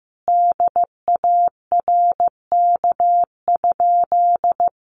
Morse code, heard as a keyed tone, about 15 wpm, 700 Hz. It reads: DARK?
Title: Sentences from Top 900 Words 15wpm